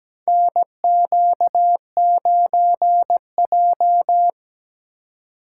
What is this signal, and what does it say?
Morse code, heard as a keyed tone, about 17 wpm, 700 Hz. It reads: NQ9J
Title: USA Call Signs - 2x Character Spacing - 17wpm